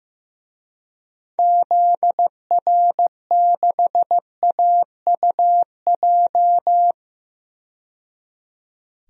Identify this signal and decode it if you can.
Morse code, heard as a keyed tone, about 15 wpm, 700 Hz. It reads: ZR6AUJ
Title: DXCC-100 Call Signs - 6 Characters - 15wpm